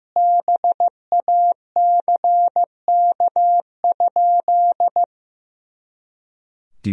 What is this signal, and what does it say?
Morse code, heard as a keyed tone, about 15 wpm, 700 Hz. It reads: BACK?
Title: Sentences from Top 300 Words and Abbreviations 15wpm